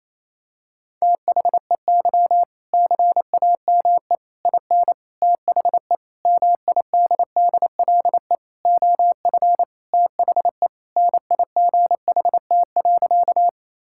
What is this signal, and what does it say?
Morse code, heard as a keyed tone, about 28 wpm, 700 Hz. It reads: THEY CAME IN THE MIDDLE OF THE NIGHT.
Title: Sentences from Top 1000 Words - Review Part 6 - 28wpm